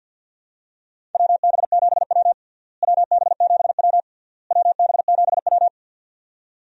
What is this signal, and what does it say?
Morse code, heard as a keyed tone, about 47 wpm, 700 Hz. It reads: WB7W WB7W WB7W
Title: USA Call Signs - 4 Characters - Repeated 3x - 50wpm